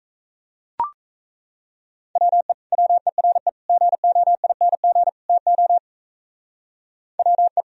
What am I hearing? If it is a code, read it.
Morse code, heard as a keyed tone, about 36 wpm, 690 Hz. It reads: WE WERE GOING TO WE